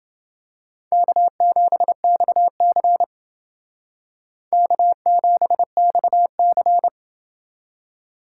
Morse code, heard as a keyed tone, about 28 wpm, 700 Hz. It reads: K7XC K7XC